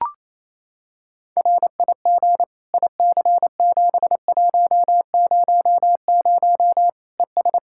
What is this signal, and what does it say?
Morse code, heard as a keyed tone, about 27 wpm, 700 Hz. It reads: RIG IC7100 ES
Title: K1USN SST QSO Practice 28wpm